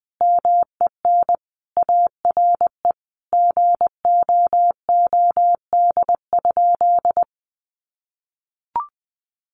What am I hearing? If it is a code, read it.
Morse code, heard as a keyed tone, about 20 wpm, 700 Hz. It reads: MEN ARE GOOD?
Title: Sentences from Top 1300 Words 20wpm